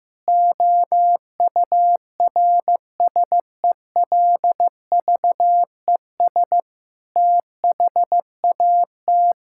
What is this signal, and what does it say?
Morse code, heard as a keyed tone, about 15 wpm, 700 Hz. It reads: OURSELVES THAT
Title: Sentences from Top 1600 Words 15wpm